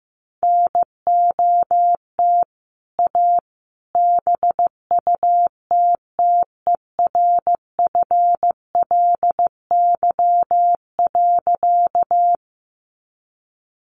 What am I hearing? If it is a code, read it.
Morse code, heard as a keyed tone, about 15 wpm, 700 Hz. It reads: NOT A BUTTERFLY.
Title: A Sound Of Thunder 15wpm